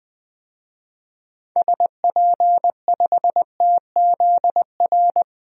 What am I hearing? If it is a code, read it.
Morse code, heard as a keyed tone, about 20 wpm, 700 Hz. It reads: SP5TZR
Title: DXCC-100 Call Signs - 6 Characters - Repeated 3x - 20wpm